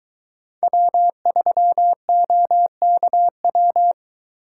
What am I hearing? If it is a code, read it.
Morse code, heard as a keyed tone, about 22 wpm, 700 Hz. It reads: W3OKW